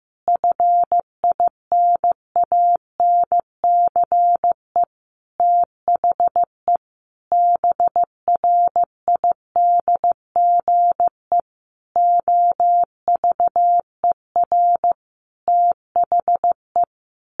Morse code, heard as a keyed tone, about 15 wpm, 700 Hz. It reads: FINANCE THE BRIDGE OVER THE
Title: Sentences from Top 1500 Words 15wpm